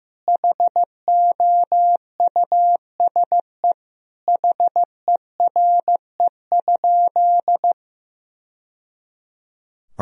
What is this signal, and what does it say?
Morse code, heard as a keyed tone, about 15 wpm, 700 Hz. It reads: HOUSE HERE?
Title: Sentences from Top 1000 Words - Review Part 1 - 15wpm